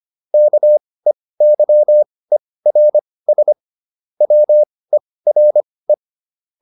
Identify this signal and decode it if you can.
Morse code, heard as a keyed tone, about 25 wpm, spaced as for 18 wpm, 600 Hz. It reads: KEYERS WERE